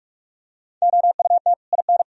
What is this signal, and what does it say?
Morse code, heard as a keyed tone, about 44 wpm, 700 Hz. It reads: OUT IN